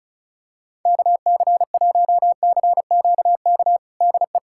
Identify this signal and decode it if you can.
Morse code, heard as a keyed tone, about 36 wpm, 700 Hz. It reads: KC1CQK DE
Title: USA Callsign DE Callsign 35wpm